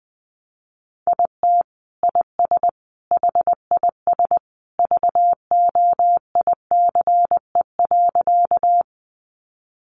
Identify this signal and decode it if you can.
Morse code, heard as a keyed tone, about 20 wpm, 700 Hz. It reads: IT IS HIS VOICE.
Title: Sentences from Top 500 Words 20wpm